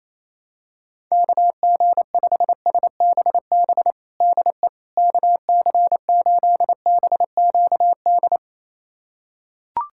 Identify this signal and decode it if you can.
Morse code, heard as a keyed tone, about 28 wpm, 700 Hz. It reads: KG5SBB DE KC8BQD
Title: USA Callsign DE Callsign - 6 Characters - 28wpm